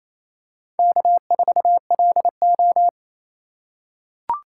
Morse code, heard as a keyed tone, about 28 wpm, 700 Hz. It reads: K4LO
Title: USA Call Signs - 3x Character Spacing - 28wpm